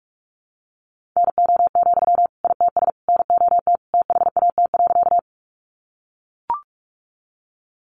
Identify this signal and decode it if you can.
Morse code, heard as a keyed tone, about 44 wpm, 700 Hz. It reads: NO, ITS NOT THAT.
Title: Sentences from Top 600 Words 45wpm